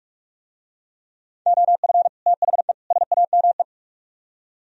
Morse code, heard as a keyed tone, about 45 wpm, 700 Hz. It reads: OF THE SAME